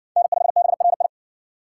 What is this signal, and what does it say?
Morse code, heard as a keyed tone, about 52 wpm, 700 Hz. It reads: N5BRI